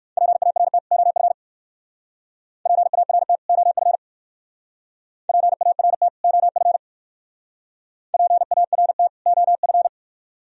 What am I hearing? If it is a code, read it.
Morse code, heard as a keyed tone, about 57 wpm, 700 Hz. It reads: PART OF PART OF PART OF PART OF